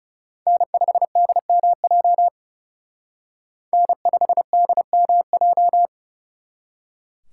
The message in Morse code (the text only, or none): N5DMJ N5DMJ